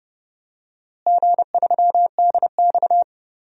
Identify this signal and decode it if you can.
Morse code, heard as a keyed tone, about 30 wpm, 700 Hz. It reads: G3DX
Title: DXCC-100 Call Signs - 4 Characters - Repeated 2x - 30wpm